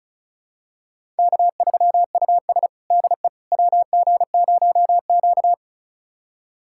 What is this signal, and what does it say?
Morse code, heard as a keyed tone, about 36 wpm, 700 Hz. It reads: K3US DE WG0Q
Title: USA Callsign DE Callsign - 4 Characters - 35wpm